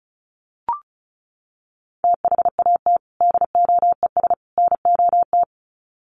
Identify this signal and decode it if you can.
Morse code, heard as a keyed tone, about 34 wpm, 700 Hz. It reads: THAT DOES NOT
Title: Sets of 3 Words 35wpm